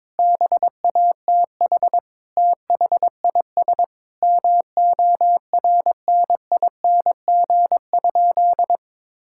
Morse code, heard as a keyed tone, about 22 wpm, 700 Hz. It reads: BATH THIS MORNING?
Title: Sentences from Top 1600 Words 22wpm